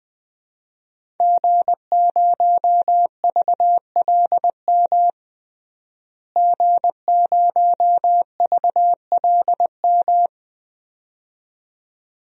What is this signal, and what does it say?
Morse code, heard as a keyed tone, about 20 wpm, 700 Hz. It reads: G0VLM G0VLM
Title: DXCC-100 Call Signs - 5 Characters - Repeated 2x - 20wpm